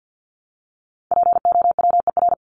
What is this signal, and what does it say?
Morse code, heard as a keyed tone, about 50 wpm, 700 Hz. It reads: POWER
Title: Top 500 Words Repeated 50wpm